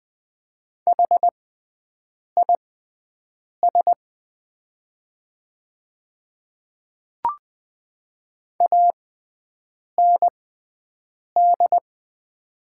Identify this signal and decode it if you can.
Morse code, heard as a keyed tone, about 20 wpm, spaced as for 7 wpm, 700 Hz. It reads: HIS AND
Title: 6x Character Spacing - Top 100 Words - 20wpm